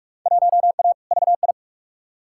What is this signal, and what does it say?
Morse code, heard as a keyed tone, about 46 wpm, 700 Hz. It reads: 1A VI
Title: Field Day Exchanges Encore - 45wpm